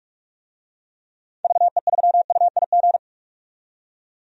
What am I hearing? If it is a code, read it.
Morse code, heard as a keyed tone, about 46 wpm, 700 Hz. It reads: VE3UIG
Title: DXCC-100 Call Signs - 6 Characters - 45wpm